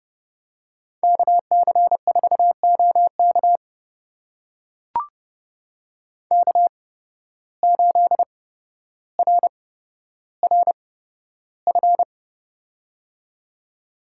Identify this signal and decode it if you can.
Morse code, heard as a keyed tone, about 30 wpm, spaced as for 13 wpm, 700 Hz. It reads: KC4OK K8RRF
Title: USA Call Signs - 5 Characters - 8x Character Spacing - 30wpm